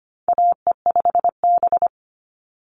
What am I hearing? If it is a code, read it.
Morse code, heard as a keyed tone, about 25 wpm, 700 Hz. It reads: AE5B
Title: NA SOTA Chaser Callsigns 25wpm